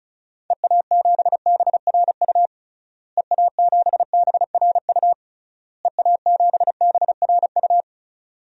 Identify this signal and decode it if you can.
Morse code, heard as a keyed tone, about 36 wpm, 700 Hz. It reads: EA7BRU EA7BRU EA7BRU
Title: DXCC-100 Call Signs - 6 Characters - Repeated 3x - 35wpm